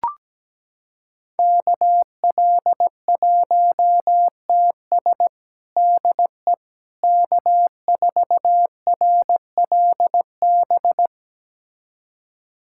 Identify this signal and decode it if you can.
Morse code, heard as a keyed tone, about 17 wpm, 700 Hz. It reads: KL1TS DE K4RLB
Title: USA Callsign DE Callsign 17wpm